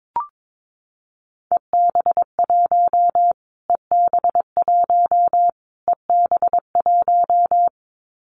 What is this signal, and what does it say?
Morse code, heard as a keyed tone, about 22 wpm, 700 Hz. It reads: EB1 EB1 EB1
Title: Callsign Prefixes for Top 25 DXCC - Repeated 3x - 22wpm